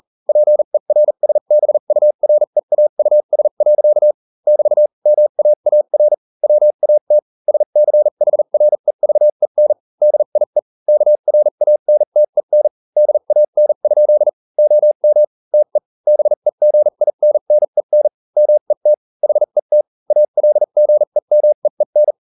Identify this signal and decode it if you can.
Morse code, heard as a keyed tone, about 40 wpm, spaced as for 35 wpm, 600 Hz. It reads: PERSBUREAUS. <BT> MAAR WAT SCHREVEN DIE KRANTEN DAN? OM TE BEGINNEN MET HET ALGEMEEN